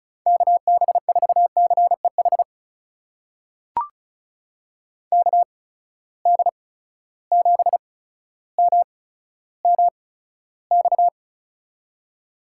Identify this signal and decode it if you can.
Morse code, heard as a keyed tone, about 36 wpm, spaced as for 15 wpm, 700 Hz. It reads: KB4CEH KD7MMX
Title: USA Call Signs - 6 Characters - 8x Character Spacing - 35wpm